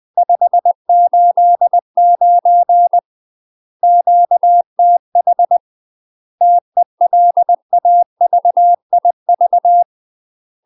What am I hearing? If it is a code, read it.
Morse code, heard as a keyed tone, about 20 wpm, 700 Hz. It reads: 589 QTH TELAVIV
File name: UK MASTER QSOs TRAINING / 20WPM / UK MASTER QSOs 0036